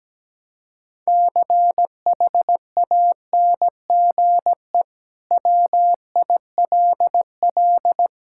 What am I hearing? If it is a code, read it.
Morse code, heard as a keyed tone, about 17 wpm, 700 Hz. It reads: CHANGE WILL
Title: Sentences from Top 1000 Words - Review Part 5 - 17wpm